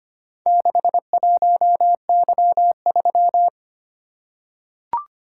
Morse code, heard as a keyed tone, about 25 wpm, 700 Hz. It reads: 61Y3
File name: practice / 25wpm / Four Letter-Number Combinations - 25wpm